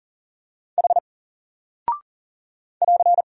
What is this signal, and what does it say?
Morse code, heard as a keyed tone, about 39 wpm, 700 Hz. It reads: H <AR>
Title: Single Character (Letter:Number:Punctuation:Prosign) 40wpm